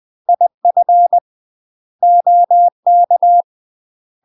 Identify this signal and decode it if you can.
Morse code, heard as a keyed tone, about 20 wpm, 700 Hz. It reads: IF OK
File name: UK MASTER QSOs TRAINING / 20WPM / UK MASTER QSOs 0055